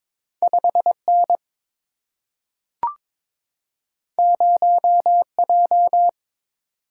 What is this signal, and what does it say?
Morse code, heard as a keyed tone, about 22 wpm, 700 Hz. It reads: 5 N 0 J
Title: Single-Letter Single-Number 22wpm